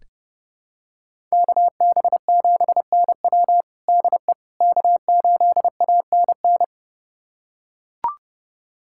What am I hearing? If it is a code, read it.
Morse code, heard as a keyed tone, about 30 wpm, 700 Hz. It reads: KB7NW DE K8ANN
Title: USA Callsign DE Callsign - 5 Characters - 30wpm